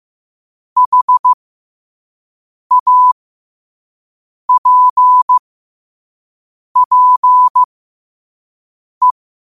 Morse code, heard as a keyed tone, about 15 wpm, spaced as for 5 wpm, 1000 Hz. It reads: HAPPE